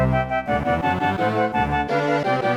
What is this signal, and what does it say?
There is no code.